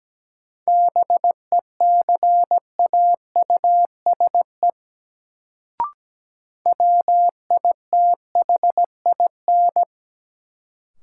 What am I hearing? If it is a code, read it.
Morse code, heard as a keyed tone, about 17 wpm, 700 Hz. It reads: BECAUSE WITHIN
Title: Sets of 2 Words - New Word Drill - 17wpm